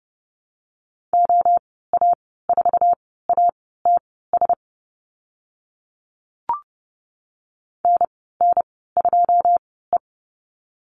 Morse code, heard as a keyed tone, about 30 wpm, spaced as for 17 wpm, 700 Hz. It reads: OA4ATS NN2E
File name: Lesson 14g - Number 2 - Call Signs 30wpm